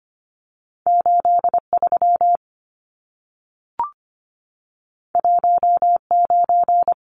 Morse code, heard as a keyed tone, about 25 wpm, 700 Hz. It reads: 83 19
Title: Numbers - 2-Digit 25wpm